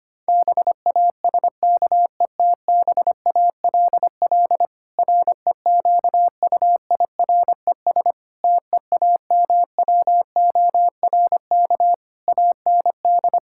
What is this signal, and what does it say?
Morse code, heard as a keyed tone, about 25 wpm, 700 Hz. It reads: BASKETBALL REQUIRES TEAMWORK AND
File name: practice / 25wpm / Difficult Sentences with Hints - 25wpm